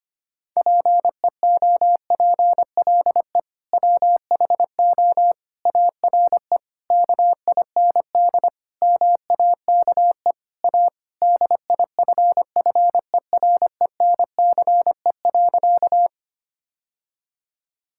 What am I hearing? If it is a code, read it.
Morse code, heard as a keyed tone, about 25 wpm, 700 Hz. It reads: PEOPLE WHO ARE KIND MAKE A DIFFERENCE.
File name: practice / 25wpm / Sentences with 3-Word Phrases - 25wpm